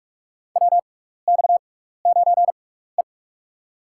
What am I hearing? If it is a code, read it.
Morse code, heard as a keyed tone, about 47 wpm, spaced as for 16 wpm, 700 Hz. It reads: WX9E
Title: USA Call Signs - 4 Characters - 6x Character Spacing - 45wpm